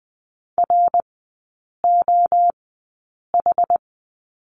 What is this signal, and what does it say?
Morse code, heard as a keyed tone, about 20 wpm, spaced as for 8 wpm, 700 Hz. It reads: ROH
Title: 067 Lesson 12e - Letter d - Three Characters 20wpm